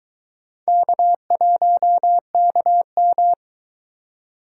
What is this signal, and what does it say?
Morse code, heard as a keyed tone, about 23 wpm, 700 Hz. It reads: K1KM